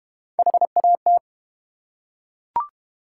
Morse code, heard as a keyed tone, about 32 wpm, 690 Hz. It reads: HAT